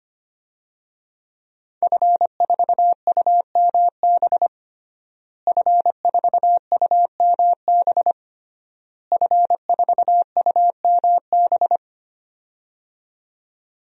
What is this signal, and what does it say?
Morse code, heard as a keyed tone, about 25 wpm, 700 Hz. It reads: F4UMB F4UMB F4UMB
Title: DXCC-100 Call Signs - 5 Characters - Repeated 3x - 25wpm